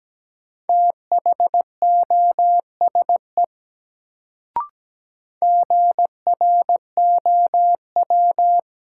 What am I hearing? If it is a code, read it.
Morse code, heard as a keyed tone, about 17 wpm, 700 Hz. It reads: THOSE GROW
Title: Sentences from Top 300 Words - New Word Drill - 17wpm